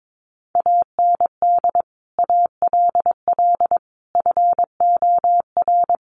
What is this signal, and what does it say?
Morse code, heard as a keyed tone, about 22 wpm, 700 Hz. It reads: AND ALL FOR
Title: Sentences from Top 100 Words 22wpm